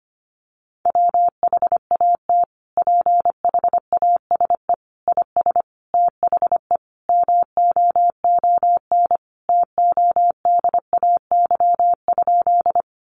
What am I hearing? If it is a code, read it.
Morse code, heard as a keyed tone, about 25 wpm, 700 Hz. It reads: WHAT PHASE IS THE MOON TODAY?
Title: Sentences from Top 1200 Words 25wpm